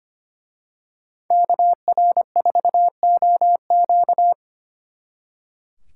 Morse code, heard as a keyed tone, about 29 wpm, 700 Hz. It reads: KR4OQ
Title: Speed Racing - USA 2x2 Call Signs 25wpm